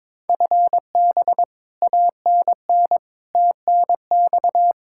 Morse code, heard as a keyed tone, about 22 wpm, 700 Hz. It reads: FB ANN TNX
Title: Practice QSO - Exchange 3 - More Info - 22wpm